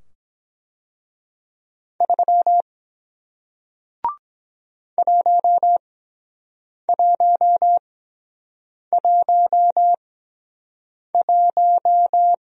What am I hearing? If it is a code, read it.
Morse code, heard as a keyed tone, about 24 wpm, spaced as for 7 wpm, 700 Hz. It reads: 3 1111